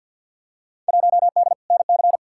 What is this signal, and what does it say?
Morse code, heard as a keyed tone, about 53 wpm, 700 Hz. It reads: 1D NC
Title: Field Day Exchanges Encore - 50wpm